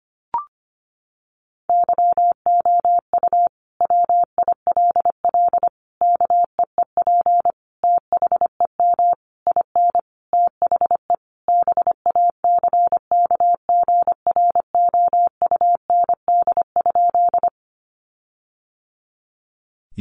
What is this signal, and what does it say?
Morse code, heard as a keyed tone, about 25 wpm, 700 Hz. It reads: YOU WILL KEEP THEM IN THE BACKGROUND?
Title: Sentences from Top 1300 Words 25wpm